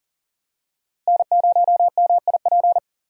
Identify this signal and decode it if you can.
Morse code, heard as a keyed tone, about 38 wpm, 700 Hz. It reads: N0MIP
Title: USA Call Signs - 5 Characters - Repeated 2x - 40wpm